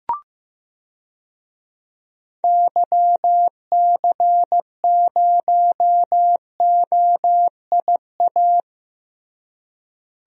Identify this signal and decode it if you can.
Morse code, heard as a keyed tone, about 15 wpm, 700 Hz. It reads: YC0OIA
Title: DXCC-100 Call Signs - 6 Characters - Repeated 2x - 15wpm